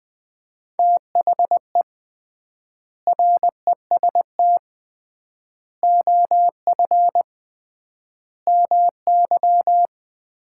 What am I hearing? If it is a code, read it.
Morse code, heard as a keyed tone, about 20 wpm, 700 Hz. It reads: THE REST OF MY
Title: 3x Word-Spacing - Sets of 5 Words - 20wpm